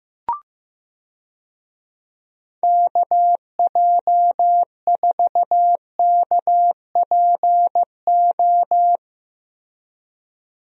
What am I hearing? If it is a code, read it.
Morse code, heard as a keyed tone, about 15 wpm, 700 Hz. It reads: KJ4KPO
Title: USA Call Signs - 6 Characters - Repeated 2x - 15wpm